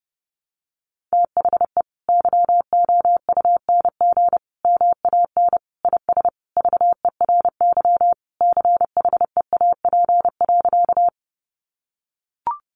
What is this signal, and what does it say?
Morse code, heard as a keyed tone, about 30 wpm, 700 Hz. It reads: THE YOUNG MAN IS VERY CHEAP.
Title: Sentences from Top 1200 Words 30wpm